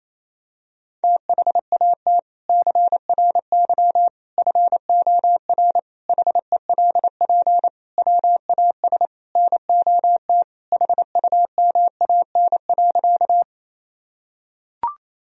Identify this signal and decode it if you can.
Morse code, heard as a keyed tone, about 28 wpm, 700 Hz. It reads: THAT CRY FOR HELP WAS NOT HUMAN.
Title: Sentences from Top 1000 Words - Review Part 3 - 28wpm